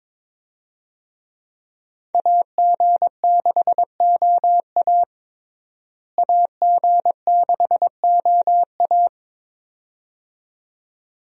Morse code, heard as a keyed tone, about 22 wpm, 700 Hz. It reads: AG6OA AG6OA
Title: USA Call Signs - 5 Characters - Repeated 2x - 22wpm